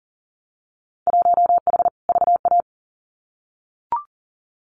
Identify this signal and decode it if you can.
Morse code, heard as a keyed tone, about 40 wpm, 700 Hz. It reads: 1H VA
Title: Winter Field Day Exchanges 40wpm